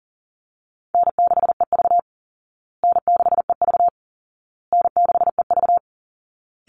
Morse code, heard as a keyed tone, about 40 wpm, 700 Hz. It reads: N6EV N6EV N6EV
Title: Call Signs Repeated 2x - 40wpm